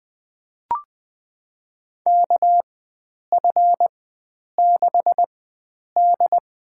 Morse code, heard as a keyed tone, about 20 wpm, spaced as for 9 wpm, 700 Hz. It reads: KF6D
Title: USA Call Signs - 4x Character Spacing - 20wpm